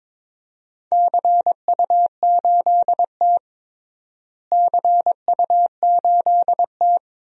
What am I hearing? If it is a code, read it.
Morse code, heard as a keyed tone, about 22 wpm, 700 Hz. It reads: CU8T CU8T
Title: DXCC-100 Call Signs - 4 Characters - Repeated 3x - 22wpm